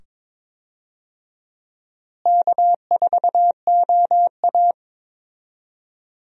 Morse code, heard as a keyed tone, about 22 wpm, 700 Hz. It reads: K4OA
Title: USA Call Signs - 4 Characters - Repeated 2x - 22wpm